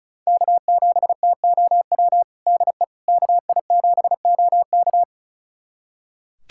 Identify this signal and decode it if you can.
Morse code, heard as a keyed tone, about 36 wpm, 700 Hz. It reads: K7TOW DE KI7OK
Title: USA Callsign DE Callsign - 5 Characters - 35wpm